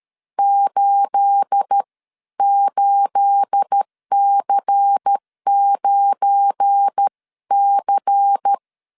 Morse code, heard as a keyed tone, about 13 wpm, spaced as for 11 wpm, 800 Hz. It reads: 88C9C
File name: CW / DET / 1 9 1500C S B 12wpm 3819